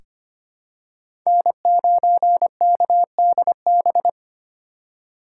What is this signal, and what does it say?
Morse code, heard as a keyed tone, about 25 wpm, 700 Hz. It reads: N9KDB
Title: K1USN SST QSO Practice - 25wpm